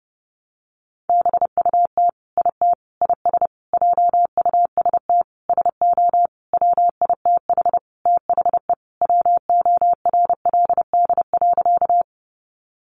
Morse code, heard as a keyed tone, about 28 wpm, 700 Hz. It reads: BUT IT IS JUST SO WITH THE WORLD.
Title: Sentences from Top 200 Words 30wpm